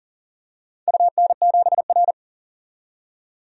Morse code, heard as a keyed tone, about 39 wpm, 700 Hz. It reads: UN7R
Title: DXCC-100 Call Signs - 4 Characters - 40wpm